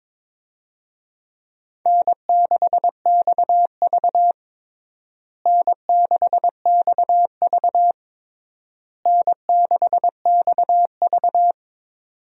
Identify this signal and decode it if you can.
Morse code, heard as a keyed tone, about 22 wpm, 700 Hz. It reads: N6XV N6XV N6XV
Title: USA Call Signs - 4 Characters - Repeated 3x - 22wpm